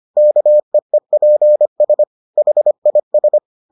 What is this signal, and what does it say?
Morse code, heard as a keyed tone, about 25 wpm, 600 Hz. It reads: KEEPS HIS